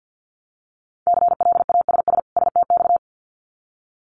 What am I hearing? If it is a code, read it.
Morse code, heard as a keyed tone, about 50 wpm, 700 Hz. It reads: CLASS STX
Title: Winter Field Day Exchanges 50wpm